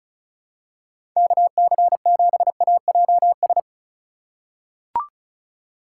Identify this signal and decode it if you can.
Morse code, heard as a keyed tone, about 36 wpm, 700 Hz. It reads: KC7AJS
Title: USA Call Signs - 6 Characters - 3x Character Spacing - 35wpm